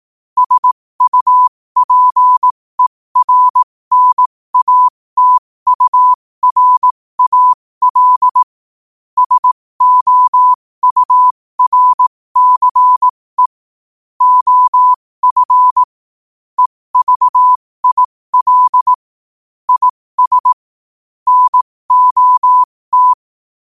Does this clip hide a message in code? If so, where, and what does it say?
Yes: Morse code from 0.4 to 23.1 s, heard as a keyed tone, about 18 wpm, spaced as for 15 wpm, 1000 Hz. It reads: SUPERNATURAL SOURCE OF EVIL IS NOT